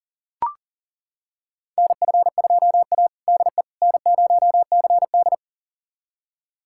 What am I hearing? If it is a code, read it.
Morse code, heard as a keyed tone, about 40 wpm, 700 Hz. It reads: NF2A DE N0CD
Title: USA Callsign DE Callsign - 4 Characters - 40wpm